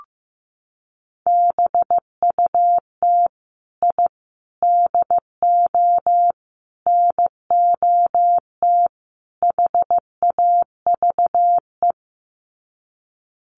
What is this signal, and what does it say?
Morse code, heard as a keyed tone, about 15 wpm, 700 Hz. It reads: BUT I DO NOT HAVE
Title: Sets of 5 Words Encore 15wpm